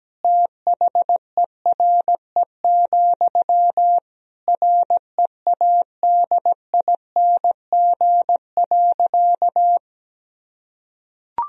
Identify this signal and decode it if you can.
Morse code, heard as a keyed tone, about 17 wpm, 700 Hz. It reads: THERE, READING.
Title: Sentences from Top 1000 Words - Review Part 3 - 17wpm